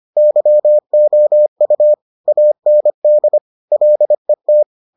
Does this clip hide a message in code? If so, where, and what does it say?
Yes: Morse code from 0.2 to 4.6 s, heard as a keyed tone, about 25 wpm, 600 Hz. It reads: YOU AND LET